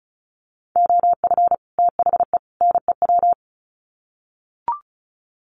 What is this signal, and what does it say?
Morse code, heard as a keyed tone, about 34 wpm, 700 Hz. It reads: OF THE NEW
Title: Sets of 3 Words 35wpm